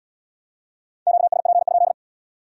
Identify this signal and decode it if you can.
Morse code, heard as a keyed tone, about 67 wpm, 700 Hz. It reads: QSP?